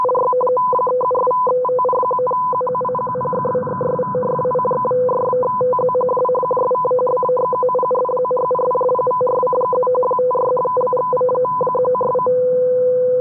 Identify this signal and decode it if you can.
Two-tone teleprinter signal, about 500 Hz shift.